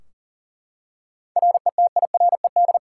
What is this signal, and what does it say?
Morse code, heard as a keyed tone, about 40 wpm, 700 Hz. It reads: RETIRED
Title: Practice QSO - Exchange 3 - More Info - 40wpm